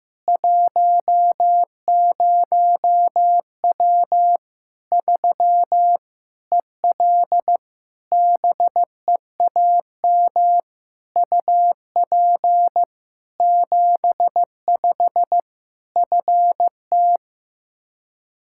Morse code, heard as a keyed tone, about 15 wpm, 700 Hz. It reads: 10W 3 EL BEAM UP 75 FT
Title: K1USN SST QSO Practice 15wpm